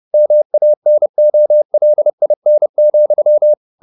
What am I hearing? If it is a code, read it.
Morse code, heard as a keyed tone, about 30 wpm, 600 Hz. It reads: MANOLIN,
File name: the old man and the sea / the old man and the sea 1 30wpm